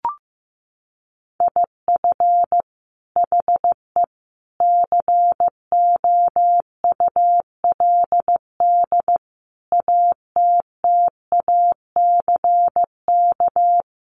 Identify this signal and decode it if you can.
Morse code, heard as a keyed tone, about 15 wpm, 700 Hz. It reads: IF HE COULD ATTACK
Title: Sentences from Top 1500 Words 15wpm